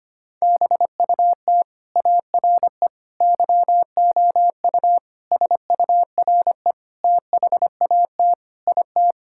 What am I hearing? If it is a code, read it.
Morse code, heard as a keyed tone, about 25 wpm, 700 Hz. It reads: BUT ARE YOU SURE THAT IT
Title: Sentences from Top 1600 Words 25wpm